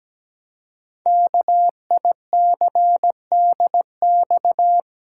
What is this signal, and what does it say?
Morse code, heard as a keyed tone, about 17 wpm, 700 Hz. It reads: KICDX
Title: Code Groups - 5 Letter:Numbers - 17wpm